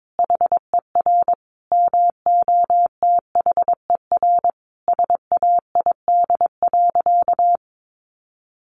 Morse code, heard as a keyed tone, about 22 wpm, 700 Hz. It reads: HER MOTHER SAID.
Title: Sentences from Top 1500 Words 22wpm